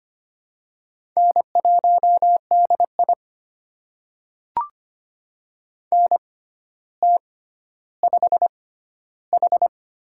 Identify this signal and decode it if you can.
Morse code, heard as a keyed tone, about 25 wpm, spaced as for 13 wpm, 700 Hz. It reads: N1DI NT5H
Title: USA Call Signs - 4 Characters - 6x Character Spacing - 25wpm